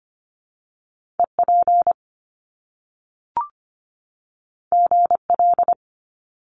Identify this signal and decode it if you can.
Morse code, heard as a keyed tone, about 25 wpm, 700 Hz. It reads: EP GL
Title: Top Two Letter Combinations in Words 25wpm